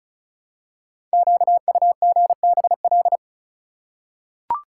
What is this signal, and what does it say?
Morse code, heard as a keyed tone, about 36 wpm, 700 Hz. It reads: QUGBL
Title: Code Groups - 5 Letters - 35wpm